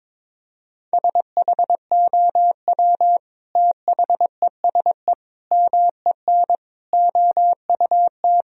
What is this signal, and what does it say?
Morse code, heard as a keyed tone, about 22 wpm, 700 Hz. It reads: SHOW THESE MEN OUT
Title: Sentences from Top 1000 Words - Review Part 1 - 22wpm